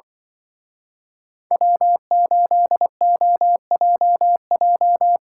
Morse code, heard as a keyed tone, about 24 wpm, 700 Hz. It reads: W8OJJ
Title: Speed Racing - USA 1x3 Call Signs 20wpm